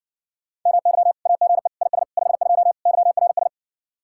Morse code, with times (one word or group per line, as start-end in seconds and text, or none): 0.7-1.1 MY
1.3-1.7 AGE
1.8-2.0 IS
2.2-2.7 52
2.9-3.5 YRS